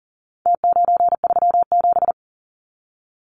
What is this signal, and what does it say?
Morse code, heard as a keyed tone, about 40 wpm, 700 Hz. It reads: T937
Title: Numbers - Contest Style - 4-Digit 40wpm